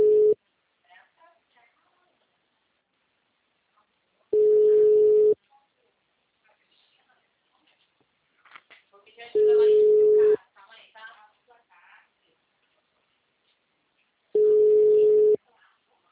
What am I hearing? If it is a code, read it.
Ringback tone (Europe, 425 Hz), with speech.